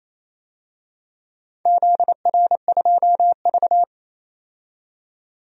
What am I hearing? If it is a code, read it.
Morse code, heard as a keyed tone, about 28 wpm, 700 Hz. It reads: ZR2V
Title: DXCC-100 Call Signs - 4 Characters - 28wpm